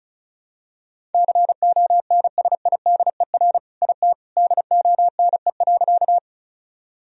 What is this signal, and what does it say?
Morse code, heard as a keyed tone, about 34 wpm, 700 Hz. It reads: CONSIDER IT DONE.
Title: Sentences from Top 1000 Words - Review Part 4 - 35wpm